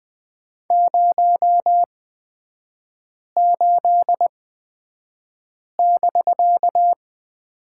Morse code, heard as a keyed tone, about 20 wpm, spaced as for 5 wpm, 700 Hz. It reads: 08<BK>
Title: 250 Lesson 40e - BK - Three Characters 20wpm